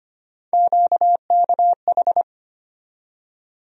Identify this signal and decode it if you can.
Morse code, heard as a keyed tone, about 25 wpm, 700 Hz. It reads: QKH
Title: USA Call Sign Postfixes - 3 Letters 25wpm